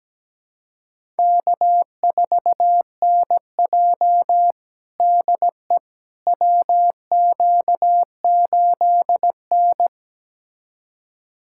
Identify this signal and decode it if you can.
Morse code, heard as a keyed tone, about 17 wpm, 700 Hz. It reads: K4NJ DE WQ8N
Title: USA Callsign DE Callsign - 4 Characters - 17wpm